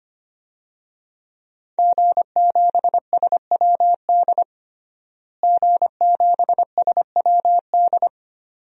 Morse code, heard as a keyed tone, about 25 wpm, 700 Hz. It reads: G7SWD G7SWD
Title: DXCC-100 Call Signs - 5 Characters - Repeated 2x - 25wpm